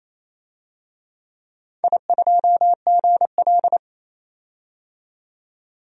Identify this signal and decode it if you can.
Morse code, heard as a keyed tone, about 27 wpm, 690 Hz. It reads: I2GL
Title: DXCC-100 Call Signs - 4 Characters - 28wpm